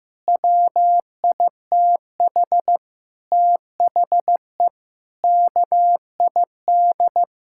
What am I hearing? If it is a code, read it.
Morse code, heard as a keyed tone, about 15 wpm, 700 Hz. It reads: WITH THE KID
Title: Sentences from Top 1600 Words 15wpm